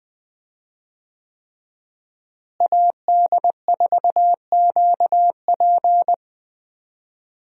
Morse code, heard as a keyed tone, about 20 wpm, 700 Hz. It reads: AD4QP
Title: USA Call Signs - 5 Characters - Repeated 3x - 20wpm